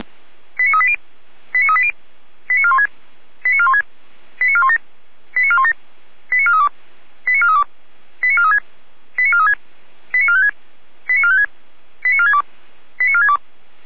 Five-tone selective call (ZVEI), heard as quick runs of five tones, tones of about 70 ms. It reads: 89380 89380 89416 89416 89417 89417 89432 89432 89436 89436 89456 89456 89462 89462